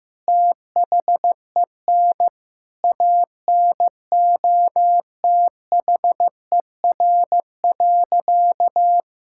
Morse code, heard as a keyed tone, about 15 wpm, 700 Hz. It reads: THEN ANOTHER.
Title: Sentences from Top 1000 Words - Review Part 6 - 15wpm